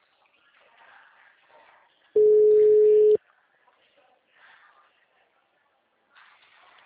Ringback tone (Europe, 425 Hz).